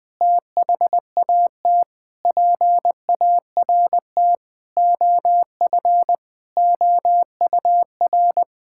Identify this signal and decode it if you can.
Morse code, heard as a keyed tone, about 20 wpm, 700 Hz. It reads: THAT PART OF OUR
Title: Sentences from Top 1000 Words - Review Part 6 - 20wpm